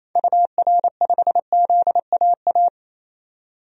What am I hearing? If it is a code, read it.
Morse code, heard as a keyed tone, about 28 wpm, 700 Hz. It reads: UR5ZAA
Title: DXCC-100 Call Signs - 6 Characters - 28wpm